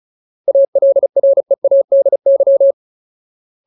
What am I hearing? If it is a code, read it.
Morse code, heard as a keyed tone, about 34 wpm, 550 Hz. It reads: ALREADY
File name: Ultimate Word List / 35WPM Ultimate